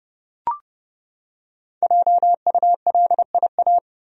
Morse code, heard as a keyed tone, about 30 wpm, 700 Hz. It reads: JULIA
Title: USA Common Names - Max 5 Letters - 30wpm